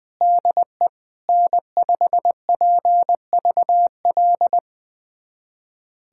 Morse code, heard as a keyed tone, about 20 wpm, 700 Hz. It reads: DE N5PVL
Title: USA Callsign DE Callsign 20wpm